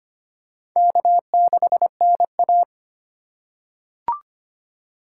Morse code, heard as a keyed tone, about 25 wpm, 700 Hz. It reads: K6NA
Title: K1USN SST QSO Practice - 25wpm